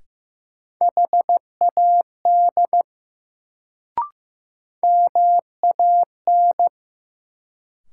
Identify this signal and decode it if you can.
Morse code, heard as a keyed tone, about 15 wpm, 700 Hz. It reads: HAD MAN
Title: Sets of 2 Words - New Word Drill - 15wpm